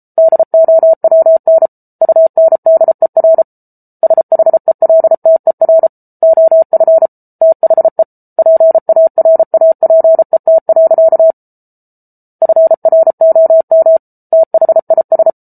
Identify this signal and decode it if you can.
Morse code, heard as a keyed tone, about 34 wpm, 650 Hz. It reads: DOWN UNDER SHELTER OF THE PARAPET. FROM THIS